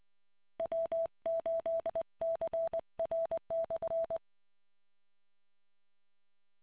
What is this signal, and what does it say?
Morse code, heard as a keyed tone, about 20 wpm, 660 Hz. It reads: W8CR/